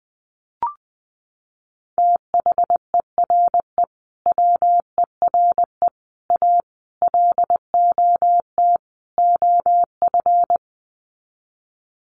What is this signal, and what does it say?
Morse code, heard as a keyed tone, about 20 wpm, 700 Hz. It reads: THERE WERE A LOT OF
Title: Sets of 5 Words 20wpm